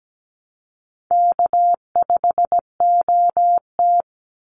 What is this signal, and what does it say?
Morse code, heard as a keyed tone, about 17 wpm, 700 Hz. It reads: K5OT
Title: NA SOTA Chaser Callsigns 17wpm